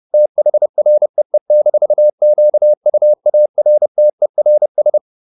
Morse code, heard as a keyed tone, about 30 wpm, 600 Hz. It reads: THREE-QUARTERS